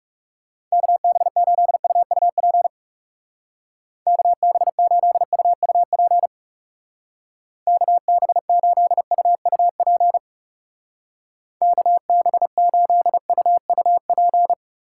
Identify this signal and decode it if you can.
Morse code, heard as a keyed tone, about 36 wpm, 700 Hz. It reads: KB8UUP KB8UUP KB8UUP KB8UUP
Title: Speed Racing - USA 2x3 Call Signs 30wpm